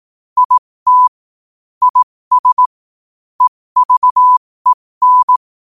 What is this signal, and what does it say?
Morse code, heard as a keyed tone, about 18 wpm, spaced as for 15 wpm, 1000 Hz. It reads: IT IS EVEN
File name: qotdfiles / 15wpm